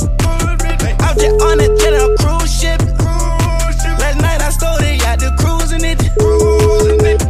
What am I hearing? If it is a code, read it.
Ringback tone (Europe, 425 Hz), with noise.